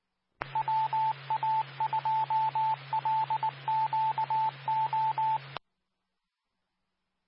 Morse code, heard as a keyed tone, about 19 wpm, 880 Hz, with radio noise. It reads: WA2LQO
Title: Nassau and Suffolk County Amateur Repeaters